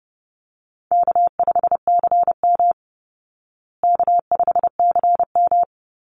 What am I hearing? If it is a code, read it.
Morse code, heard as a keyed tone, about 30 wpm, 700 Hz. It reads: K5CM K5CM
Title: Easy Call Signs Repeated 3x - 30wpm